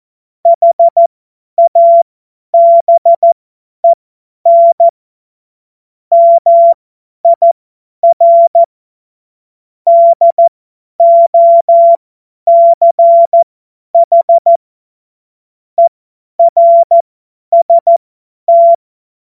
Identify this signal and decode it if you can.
Morse code, heard as a keyed tone, about 14 wpm, spaced as for 10 wpm, 680 Hz. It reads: HABEN MIR DOCH ERST
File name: Morsekurs / Witze / Witz 16. Mai 2023 50 F70